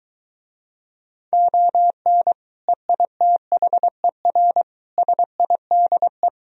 Morse code, heard as a keyed tone, about 23 wpm, 700 Hz. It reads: ON EITHER SIDE